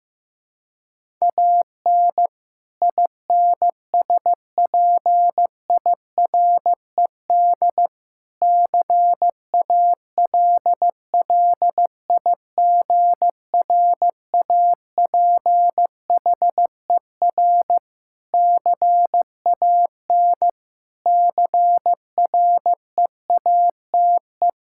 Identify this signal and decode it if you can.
Morse code, heard as a keyed tone, about 15 wpm, 700 Hz. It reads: AN INSPIRED CALLIGRAPHER CAN CREATE
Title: Pangram Challenge v2 - 15wpm